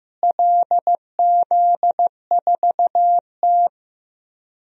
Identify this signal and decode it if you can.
Morse code, heard as a keyed tone, about 15 wpm, 700 Hz. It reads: LZ4T
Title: DXCC-100 Call Signs - 4 Characters - Repeated 3x - 15wpm